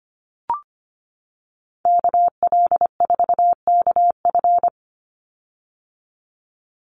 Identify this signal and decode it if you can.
Morse code, heard as a keyed tone, about 25 wpm, 700 Hz. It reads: KL4KF
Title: USA Call Signs 25wpm